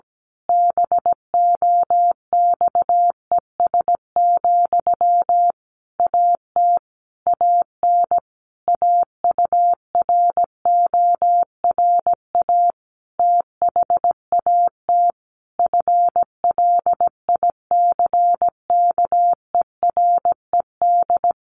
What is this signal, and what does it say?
Morse code, heard as a keyed tone, about 17 wpm, 700 Hz. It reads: BOXES, AT AN AURORA THAT FLICKERED
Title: A Sound Of Thunder 17wpm